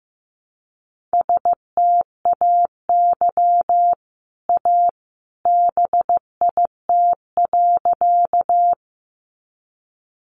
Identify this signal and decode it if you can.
Morse code, heard as a keyed tone, about 15 wpm, 700 Hz. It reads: STAY A BIT.
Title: Sentences from Top 900 Words 15wpm